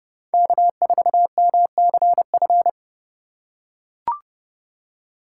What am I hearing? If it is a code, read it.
Morse code, heard as a keyed tone, about 30 wpm, 700 Hz. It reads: K4MCF
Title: USA Call Signs - 5 Characters - 3x Character Spacing - 30wpm